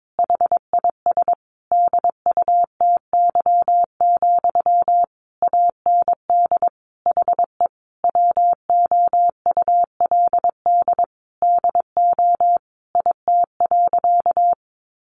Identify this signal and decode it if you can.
Morse code, heard as a keyed tone, about 22 wpm, 700 Hz. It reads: HIS DUTY, AND HE WOULD DO IT.
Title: Sentences from Top 1500 Words 22wpm